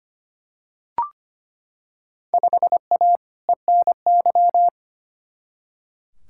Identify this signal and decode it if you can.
Morse code, heard as a keyed tone, about 25 wpm, 700 Hz. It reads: 5A ENY
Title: Field Day Exchanges - 25wpm